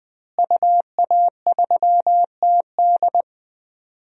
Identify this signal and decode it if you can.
Morse code, heard as a keyed tone, about 20 wpm, 700 Hz. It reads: UA3TD
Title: DXCC-100 Call Signs - 5 Characters - Repeated 3x - 20wpm